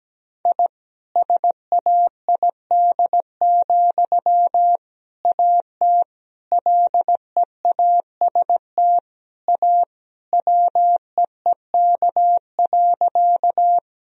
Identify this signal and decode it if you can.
Morse code, heard as a keyed tone, about 17 wpm, 700 Hz. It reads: I SAID, AT LEAST A WEEK.
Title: Sentences from Top 1000 Words - Review Part 2 - 17wpm